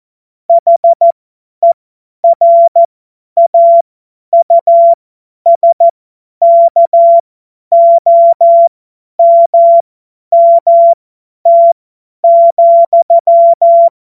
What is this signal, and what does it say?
Morse code, heard as a keyed tone, about 14 wpm, spaced as for 10 wpm, 680 Hz. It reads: HERAUSKOMMT,
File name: Witz-05-August-2025-50-F70